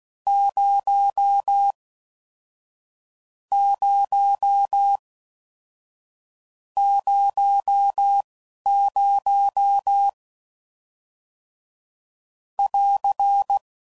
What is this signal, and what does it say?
Morse code, heard as a keyed tone, about 16 wpm, spaced as for 11 wpm, 800 Hz. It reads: 0 0 00 <AR>